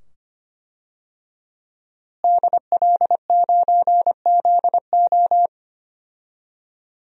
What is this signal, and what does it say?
Morse code, heard as a keyed tone, about 25 wpm, 700 Hz. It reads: DL9ZO
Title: DXCC-100 Call Signs - 5 Characters - Repeated 2x - 25wpm